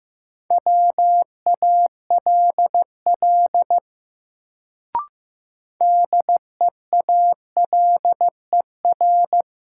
Morse code, heard as a keyed tone, about 15 wpm, 700 Hz. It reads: WALL DEALER
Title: Sentences from Top 1100 Words - New Word Drill Alt - 15wpm